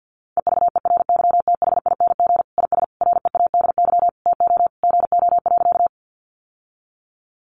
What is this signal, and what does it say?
Morse code, heard as a keyed tone, about 50 wpm, 700 Hz. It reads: EVERYTHING IS READY TO GO.
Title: Sentences from Top 700 Words 50wpm